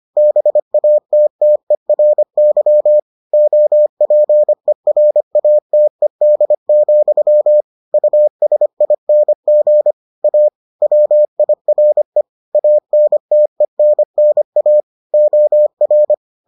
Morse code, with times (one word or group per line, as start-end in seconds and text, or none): 0.2-3.0 BATTERY
3.3-7.6 OPERATED,
7.9-9.9 USING
10.2-10.5 A
10.8-12.2 WIRE
12.6-14.8 ANTENNA
15.1-16.1 OR